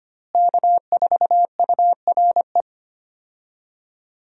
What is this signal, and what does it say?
Morse code, heard as a keyed tone, about 25 wpm, 700 Hz. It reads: K4URE
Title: USA Call Signs - 5 Characters - Repeated 3x - 25wpm